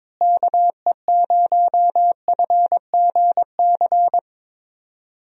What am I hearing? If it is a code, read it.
Morse code, heard as a keyed tone, about 21 wpm, 700 Hz. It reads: KE0FGC
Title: USA Call Signs - 6 Characters - Repeated 3x - 22wpm